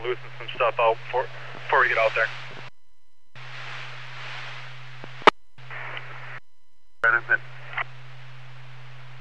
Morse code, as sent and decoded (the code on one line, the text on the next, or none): none